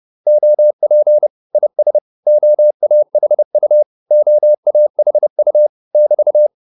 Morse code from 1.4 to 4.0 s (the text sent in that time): IS OAHU